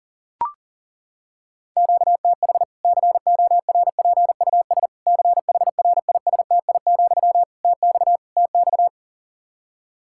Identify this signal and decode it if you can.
Morse code, heard as a keyed tone, about 37 wpm, 700 Hz. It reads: QTH CORPUS CHRISTI, TX TX
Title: Practice QSO - Exchange 1 - Beginning - 40wpm